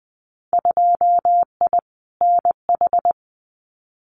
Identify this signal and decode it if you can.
Morse code, heard as a keyed tone, about 19 wpm, 700 Hz. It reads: 2I NH
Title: Winter Field Day Exchanges Encore 20wpm